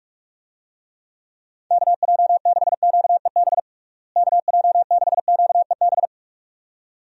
Morse code, heard as a keyed tone, about 42 wpm, 700 Hz. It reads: KJ6QEB KJ6QEB
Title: USA Call Signs - 6 Characters - Repeated 2x - 45wpm